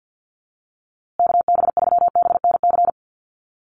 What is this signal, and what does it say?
Morse code, heard as a keyed tone, about 50 wpm, 700 Hz. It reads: KB3BNC
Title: USA Call Signs Repeated 3x 50wpm